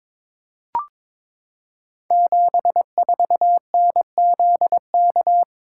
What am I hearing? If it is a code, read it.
Morse code, heard as a keyed tone, about 22 wpm, 700 Hz. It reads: 74NZK
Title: Code Groups - 5 Letter:Numbers - 22wpm